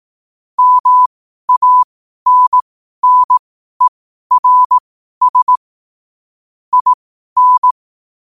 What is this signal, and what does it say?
Morse code, heard as a keyed tone, about 18 wpm, spaced as for 12 wpm, 1000 Hz. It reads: MANNERS IN
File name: qotdfiles / 12wpm